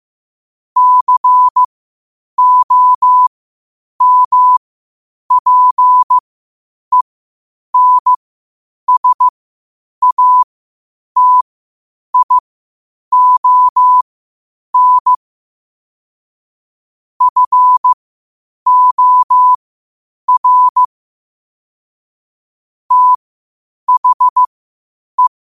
Morse code, heard as a keyed tone, about 15 wpm, spaced as for 8 wpm, 1000 Hz. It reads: COMPENSATION FOR THE
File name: qotdfiles / 8wpm